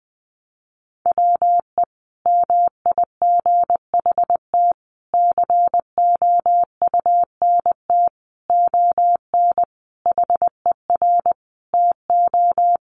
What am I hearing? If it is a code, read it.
Morse code, heard as a keyed tone, about 20 wpm, 700 Hz. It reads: WE MIGHT COUNT ON HER TO
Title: Sentences from Top 1500 Words 20wpm